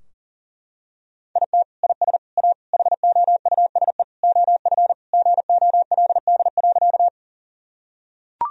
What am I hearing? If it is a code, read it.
Morse code, heard as a keyed tone, about 40 wpm, 700 Hz. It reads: IT IS A HOUSE OF GOLD.
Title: Sentences from Top 1000 Words - Review Part 3 - 40wpm